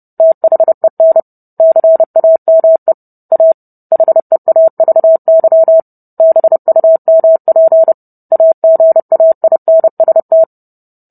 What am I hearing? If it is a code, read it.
Morse code, heard as a keyed tone, about 30 wpm, 650 Hz. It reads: THEN CAME A HEAVY BUMP AGAINST